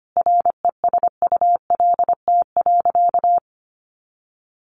Morse code, heard as a keyed tone, about 25 wpm, 700 Hz. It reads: RESULT.
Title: Sentences from Top 700 Words 25wpm